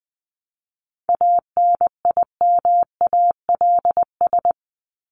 Morse code, heard as a keyed tone, about 20 wpm, 700 Hz. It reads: ANIMALS
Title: ES:S Suffix - Top 100 Words with Base Words 20wpm